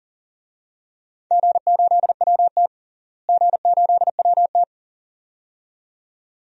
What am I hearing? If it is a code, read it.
Morse code, heard as a keyed tone, about 40 wpm, 700 Hz. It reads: G8WT G8WT
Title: DXCC-100 Call Signs - 4 Characters - Repeated 2x - 40wpm